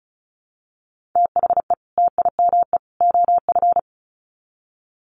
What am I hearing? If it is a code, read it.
Morse code, heard as a keyed tone, about 34 wpm, 700 Hz. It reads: THE TIME OF
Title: Sets of 3 Words 35wpm